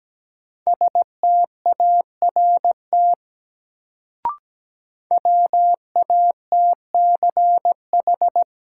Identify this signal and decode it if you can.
Morse code, heard as a keyed tone, about 17 wpm, 700 Hz. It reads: START WATCH
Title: Sentences from Top 300 Words - New Word Drill - 17wpm